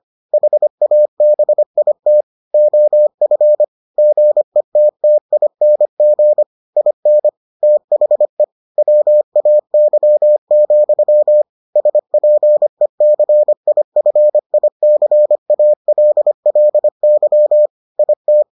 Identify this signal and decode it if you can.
Morse code, heard as a keyed tone, about 25 wpm, 600 Hz. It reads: HABIT OF GETTING IN THE WAY, SPECIFICALLY IT